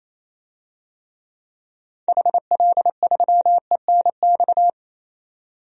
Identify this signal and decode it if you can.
Morse code, heard as a keyed tone, about 28 wpm, 700 Hz. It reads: HL3ENX
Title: DXCC-100 Call Signs - 6 Characters - 28wpm